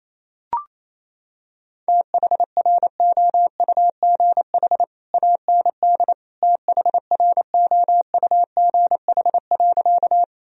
Morse code, heard as a keyed tone, about 28 wpm, 700 Hz. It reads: THROUGH AND THROUGH.
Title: Sentences from Top 1000 Words - Review Part 1 - 28wpm